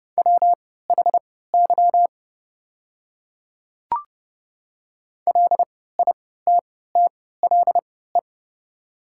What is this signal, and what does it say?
Morse code, heard as a keyed tone, about 30 wpm, spaced as for 17 wpm, 700 Hz. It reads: WHY LITTLE
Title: 3x Character Spacing - Top 200 Words - 30wpm